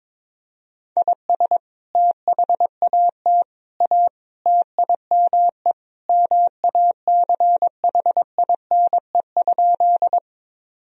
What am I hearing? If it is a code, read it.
Morse code, heard as a keyed tone, about 22 wpm, 700 Hz. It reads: IS THAT A TIME MACHINE?
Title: Sentences from Top 1000 Words - Review Part 2 - 22wpm